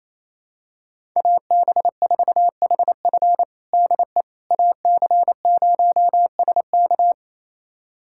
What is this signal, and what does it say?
Morse code, heard as a keyed tone, about 28 wpm, 700 Hz. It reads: AB4HF DE AC0SK
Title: USA Callsign DE Callsign - 5 Characters - 28wpm